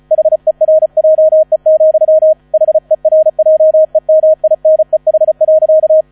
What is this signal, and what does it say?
Morse code, heard as a keyed tone, about 34 wpm, 630 Hz, with radio noise. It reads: HERJE, HERJEMINEH.
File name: 35wpm